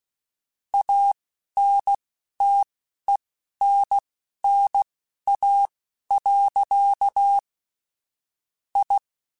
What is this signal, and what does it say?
Morse code, heard as a keyed tone, about 16 wpm, spaced as for 11 wpm, 800 Hz. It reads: ANTENNA. I